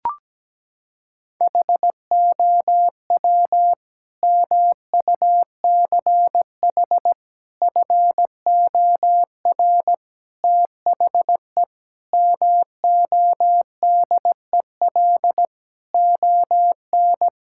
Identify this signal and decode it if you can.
Morse code, heard as a keyed tone, about 17 wpm, 700 Hz. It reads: HOW MUCH FOR THE MODEL ON
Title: Sentences from Top 1000 Words - Review Part 5 - 17wpm